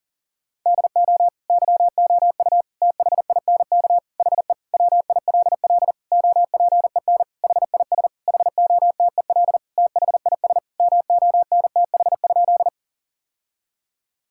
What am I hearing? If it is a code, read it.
Morse code, heard as a keyed tone, about 40 wpm, 700 Hz. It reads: DO YOU THINK HE WILL OPEN HIS HOTEL THIS MONTH?
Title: Sentences from Top 1600 Words 40wpm